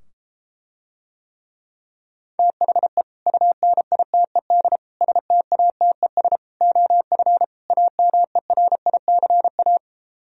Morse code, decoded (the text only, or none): THE UNITED STATES OF AMERICA